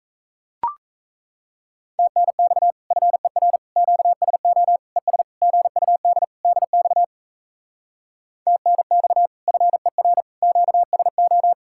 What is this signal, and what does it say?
Morse code, heard as a keyed tone, about 38 wpm, 700 Hz. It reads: TNX FER QSO ES GUD DX TNX FER QSO